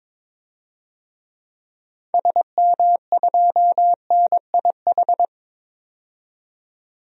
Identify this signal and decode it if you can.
Morse code, heard as a keyed tone, about 22 wpm, 700 Hz. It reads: SM2NIH
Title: DXCC-100 Call Signs - 6 Characters - 22wpm